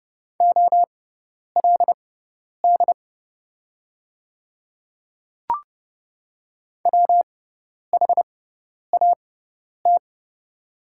Morse code, heard as a keyed tone, about 28 wpm, spaced as for 10 wpm, 700 Hz. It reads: OLD WHAT
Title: 6x Character Spacing - Top 300 Words - 30wpm